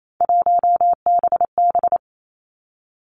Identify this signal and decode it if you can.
Morse code, heard as a keyed tone, about 28 wpm, 700 Hz. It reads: 1BB
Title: Three Challenging Letter-Number Combinations 28wpm